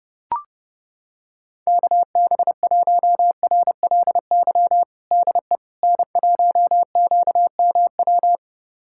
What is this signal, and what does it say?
Morse code, heard as a keyed tone, about 32 wpm, 700 Hz. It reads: KB1RLY DE N1QMW
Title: USA Callsign DE Callsign 30wpm